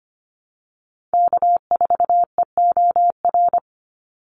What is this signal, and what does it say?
Morse code, heard as a keyed tone, about 25 wpm, 700 Hz. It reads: K4EOR
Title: Call Signs - 25wpm